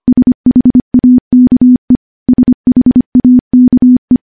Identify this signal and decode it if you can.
Morse code, heard as a keyed tone, about 25 wpm, 250 Hz. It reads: SHAKE SHAKE
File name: morse / morse-training-25WPM